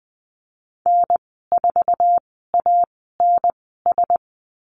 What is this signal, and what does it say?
Morse code, heard as a keyed tone, about 20 wpm, spaced as for 14 wpm, 700 Hz. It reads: N4ANS
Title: 069 Lesson 12g - Letter d - Call Signs 20wpm